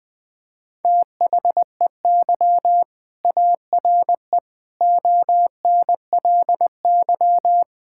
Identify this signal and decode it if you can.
Morse code, heard as a keyed tone, about 20 wpm, 700 Hz. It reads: THEY ARE ONLY
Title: Sentences from Top 1000 Words - Review Part 6 - 20wpm